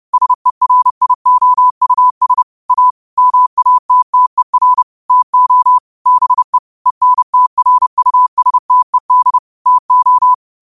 Morse code, heard as a keyed tone, about 30 wpm, 1000 Hz. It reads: SERIOUS A MATTER TO BE ENTRUSTED TO
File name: qotdfiles / 30wpm